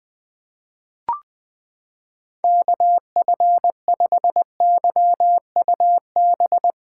Morse code, heard as a keyed tone, about 21 wpm, 700 Hz. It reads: KF5YUB
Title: K1USN SST QSO Practice - 20wpm